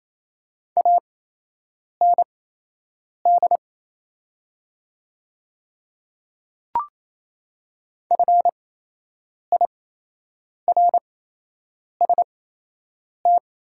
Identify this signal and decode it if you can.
Morse code, heard as a keyed tone, about 28 wpm, spaced as for 8 wpm, 700 Hz. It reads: AND FIRST
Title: 8x Character Spacing - Top 100 Words - 28wpm